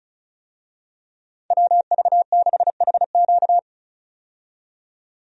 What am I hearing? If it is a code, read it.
Morse code, heard as a keyed tone, about 36 wpm, 700 Hz. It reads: WV6HQ
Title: USA Call Signs - 5 Characters - 35wpm